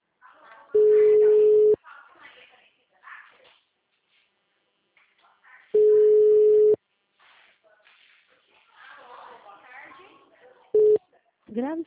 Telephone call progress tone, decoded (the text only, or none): ringback tone